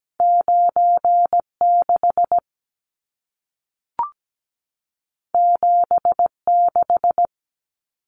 Morse code, heard as a keyed tone, about 17 wpm, 700 Hz. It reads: 96 76
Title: Numbers - 2-Digit 17wpm